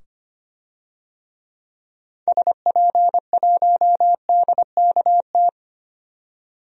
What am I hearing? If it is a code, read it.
Morse code, heard as a keyed tone, about 25 wpm, 700 Hz. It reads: SP1DKT
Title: DXCC-100 Call Signs - 6 Characters - 25wpm